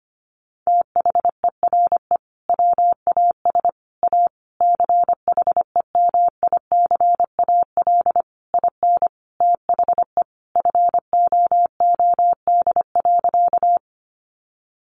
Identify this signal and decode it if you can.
Morse code, heard as a keyed tone, about 25 wpm, 700 Hz. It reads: THERE WAS A CHEMICAL IN THE FOOD.
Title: Sentences from Top 1200 Words 25wpm